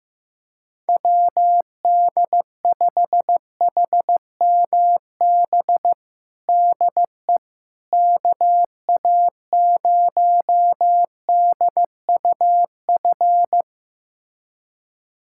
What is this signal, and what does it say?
Morse code, heard as a keyed tone, about 15 wpm, 700 Hz. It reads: WD5HMB DE KA0DUF
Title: USA Callsign DE Callsign 15wpm